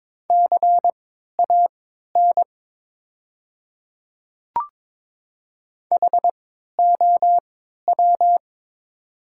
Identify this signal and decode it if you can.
Morse code, heard as a keyed tone, about 22 wpm, spaced as for 13 wpm, 700 Hz. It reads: CAN HOW
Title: 3x Character Spacing - Top 100 Words - 22wpm